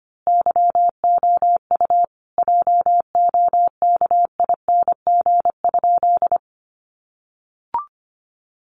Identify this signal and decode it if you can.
Morse code, heard as a keyed tone, about 25 wpm, 700 Hz. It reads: YOU JOKING?
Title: A Sound Of Thunder 25wpm